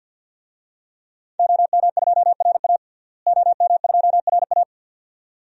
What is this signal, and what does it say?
Morse code, heard as a keyed tone, about 49 wpm, 700 Hz. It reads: OM2RA OM2RA
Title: DXCC-100 Call Signs - 5 Characters - Repeated 2x - 50wpm